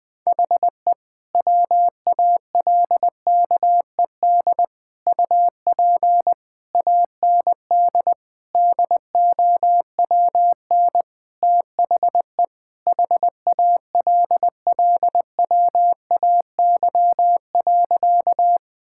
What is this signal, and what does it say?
Morse code, heard as a keyed tone, about 20 wpm, 700 Hz. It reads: HE WALKED UP AND DOWN THE HALLWAY.
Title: Sentences with 4-Word Phrases - 20wpm